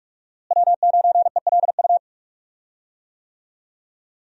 Morse code, heard as a keyed tone, about 46 wpm, 700 Hz. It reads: W9ELU
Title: USA Call Signs - 5 Characters - Repeated 2x - 45wpm